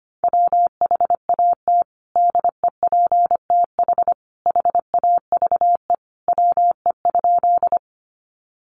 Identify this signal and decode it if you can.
Morse code, heard as a keyed tone, about 25 wpm, 700 Hz. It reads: WHAT DEPTH HAVE WE?
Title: Sentences from Top 1200 Words 25wpm